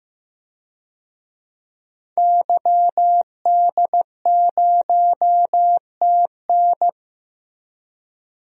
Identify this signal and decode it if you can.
Morse code, heard as a keyed tone, about 15 wpm, 700 Hz. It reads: YD0TN
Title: DXCC-100 Call Signs - 5 Characters - Repeated 3x - 15wpm